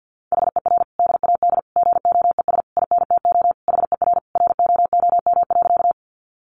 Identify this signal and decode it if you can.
Morse code, heard as a keyed tone, about 50 wpm, 700 Hz. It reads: HER DAD GOES INTO HER ROOM.
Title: Additional Easy Phrases 50wpm